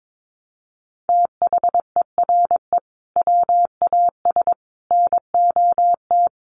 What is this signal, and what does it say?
Morse code, heard as a keyed tone, about 22 wpm, 700 Hz. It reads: THERE WAS NOT
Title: Sentences from Top 1400 Words 22wpm